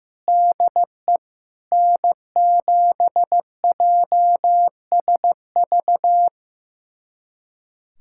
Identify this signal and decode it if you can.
Morse code, heard as a keyed tone, about 15 wpm, 700 Hz. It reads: DE N7JSV
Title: USA Callsign DE Callsign - 5 Characters - 15wpm